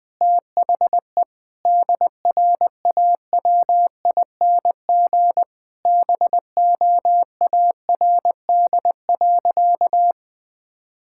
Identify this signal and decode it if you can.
Morse code, heard as a keyed tone, about 20 wpm, 700 Hz. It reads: THE DRAWING BOARD.
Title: Sentences with 7-Word Phrases - 20wpm